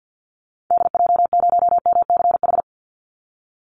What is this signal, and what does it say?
Morse code, heard as a keyed tone, about 50 wpm, 700 Hz. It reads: DJ0MCH
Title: Call Signs Repeated 2x - 50wpm